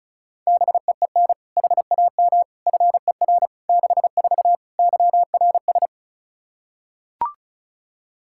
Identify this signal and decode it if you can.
Morse code, heard as a keyed tone, about 36 wpm, 700 Hz. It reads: BEEN HAM FER 64 YRS
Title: K1USN SST QSO Practice 35wpm